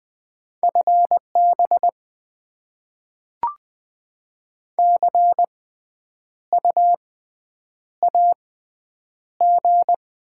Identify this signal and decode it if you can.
Morse code, heard as a keyed tone, about 20 wpm, spaced as for 9 wpm, 690 Hz. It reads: FB CU AG
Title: Top 20 QSO Elements - 6x Character Spacing - 20wpm